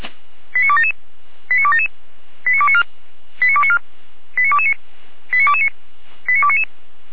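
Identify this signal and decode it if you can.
Five-tone selective call (ZVEI), heard as quick runs of five tones, tones of about 70 ms. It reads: 89277 89277 89294 89294 89229 89229 89299
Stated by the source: ZVEI five-tone selective call 89310 received on SDR.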